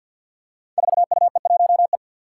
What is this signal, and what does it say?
Morse code, heard as a keyed tone, about 49 wpm, 700 Hz. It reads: 4UE1E
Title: Five Challenging Letter-Number Combinations 50wpm